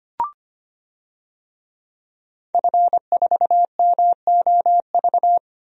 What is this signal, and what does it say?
Morse code, heard as a keyed tone, about 25 wpm, 700 Hz. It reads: F4MOV
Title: DXCC-100 Call Signs - 5 Characters - 25wpm